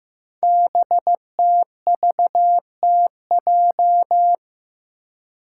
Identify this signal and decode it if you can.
Morse code, heard as a keyed tone, about 15 wpm, 700 Hz. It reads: BTVTJ
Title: Code Groups - 5 Letters - 15wpm